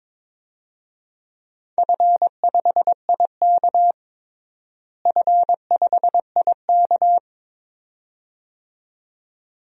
Morse code, heard as a keyed tone, about 22 wpm, 700 Hz. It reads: F5IK F5IK
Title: DXCC-100 Call Signs - 4 Characters - Repeated 2x - 22wpm